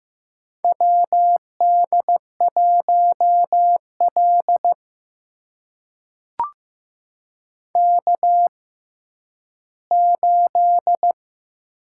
Morse code, heard as a keyed tone, about 15 wpm, spaced as for 10 wpm, 700 Hz. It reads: WD1L K8
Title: USA Call Signs - 4 Characters - 6x Character Spacing - 15wpm